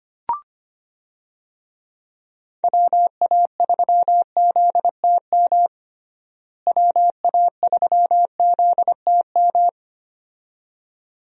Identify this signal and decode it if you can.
Morse code, heard as a keyed tone, about 25 wpm, 700 Hz. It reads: WA3ZTM WA3ZTM
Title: USA Call Signs - 6 Characters - Repeated 2x - 25wpm